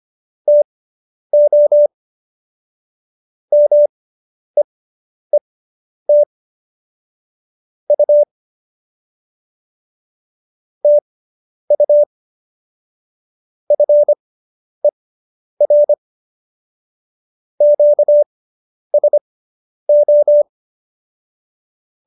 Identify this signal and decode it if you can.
Morse code, heard as a keyed tone, about 25 wpm, spaced as for 10 wpm, 590 Hz. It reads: TO MEET U TU FER QSO